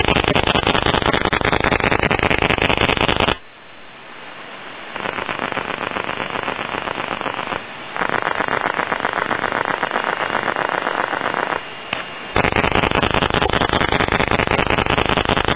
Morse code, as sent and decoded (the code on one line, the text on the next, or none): none